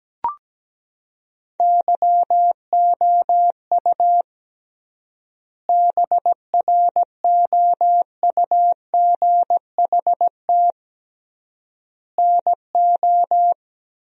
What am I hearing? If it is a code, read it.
Morse code, heard as a keyed tone, about 17 wpm, 700 Hz. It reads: YOU BROUGHT NO
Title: 3x Word-Spacing - Sentences from Top 500 Words - 17wpm